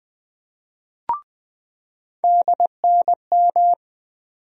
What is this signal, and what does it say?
Morse code, heard as a keyed tone, about 20 wpm, 700 Hz. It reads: DNM
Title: USA Call Sign Postfixes - 3 Letters 20wpm